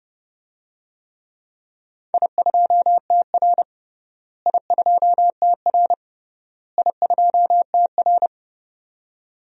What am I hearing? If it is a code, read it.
Morse code, heard as a keyed tone, about 30 wpm, 700 Hz. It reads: I2TR I2TR I2TR
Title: DXCC-100 Call Signs - 4 Characters - Repeated 3x - 30wpm